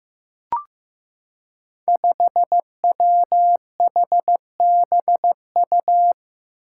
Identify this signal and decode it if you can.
Morse code, heard as a keyed tone, about 15 wpm, 700 Hz. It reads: 5WHBU
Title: Five Challenging Letter-Number Combinations 15wpm